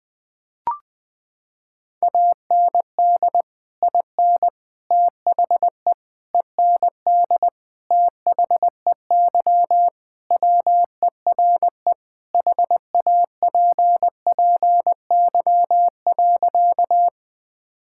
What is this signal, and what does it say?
Morse code, heard as a keyed tone, about 20 wpm, 700 Hz. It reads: AND IN THE END THEY WERE HAPPY.
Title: Sentences with 2-Word Phrases - 20wpm